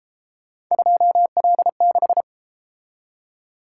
Morse code, heard as a keyed tone, about 32 wpm, 700 Hz. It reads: 2L6